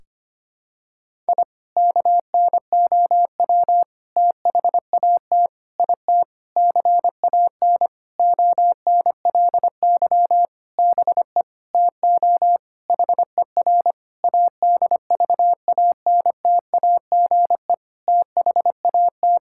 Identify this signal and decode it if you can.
Morse code, heard as a keyed tone, about 25 wpm, 700 Hz. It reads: I KNOW THAT IT CAN ONLY BE TO HER ADVANTAGE THAT
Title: Sentences from Top 1000 Words - Review Part 6 - 25wpm